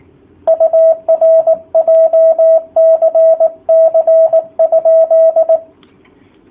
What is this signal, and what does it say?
Morse code, heard as a keyed tone, about 18 wpm, 650 Hz. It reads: UR JCC?
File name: jh8cbh / 2019cw / sp18- w